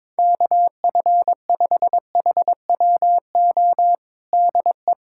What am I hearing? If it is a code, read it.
Morse code, heard as a keyed tone, about 21 wpm, 700 Hz. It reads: KF5HWO DE
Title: USA Callsign DE Callsign - 6 Characters - 22wpm